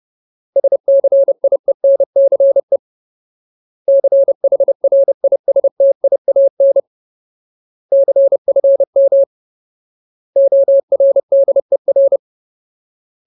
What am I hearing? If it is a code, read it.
Morse code, heard as a keyed tone, about 30 wpm, 550 Hz. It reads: SCIENCE CHRISTIAN CFM ORDER
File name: Ultimate Word List / 30WPM Ultimate